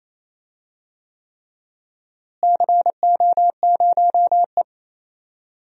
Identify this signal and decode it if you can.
Morse code, heard as a keyed tone, about 28 wpm, 700 Hz. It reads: CO0E
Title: DXCC-100 Call Signs - 4 Characters - Repeated 3x - 28wpm